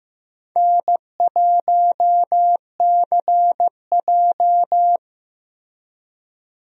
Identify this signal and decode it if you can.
Morse code, heard as a keyed tone, about 15 wpm, 700 Hz. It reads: N1CJ
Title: USA Call Signs - 4 Characters - Repeated 2x - 15wpm